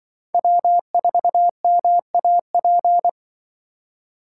Morse code, heard as a keyed tone, about 24 wpm, 700 Hz. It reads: W4MAP
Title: Speed Racing - USA 1x3 Call Signs 20wpm